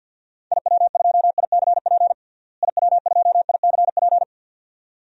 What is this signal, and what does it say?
Morse code, heard as a keyed tone, about 51 wpm, 700 Hz. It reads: IW2ICP IW2ICP
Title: DXCC-100 Call Signs - 6 Characters - Repeated 2x - 50wpm